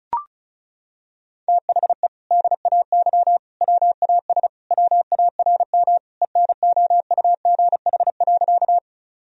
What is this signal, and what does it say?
Morse code, heard as a keyed tone, about 34 wpm, 700 Hz. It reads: THE DAY WAS WARM ENOUGH.
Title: Sentences from Top 1000 Words - Review Part 2 - 35wpm